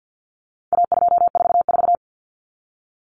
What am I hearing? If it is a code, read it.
Morse code, heard as a keyed tone, about 48 wpm, 700 Hz. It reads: A244
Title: Common Cut Numbers - 4-Digit 50wpm